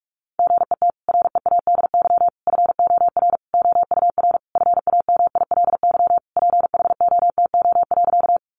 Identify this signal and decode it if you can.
Morse code, heard as a keyed tone, about 44 wpm, 700 Hz. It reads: GET READY FOR OUR FAMILY PHOTO.
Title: Sentences from Top 1200 Words 45wpm